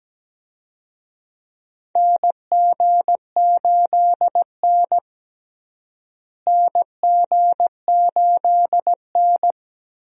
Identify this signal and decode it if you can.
Morse code, heard as a keyed tone, about 17 wpm, 700 Hz. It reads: NG8N NG8N
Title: USA Call Signs - 4 Characters - Repeated 2x - 17wpm